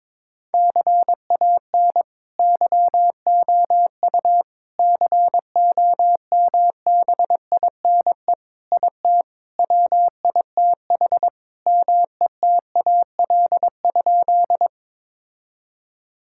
Morse code, heard as a keyed tone, about 22 wpm, 700 Hz. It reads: CAN YOU COMBINE IT WITH METAL?
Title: Sentences from Top 1600 Words 22wpm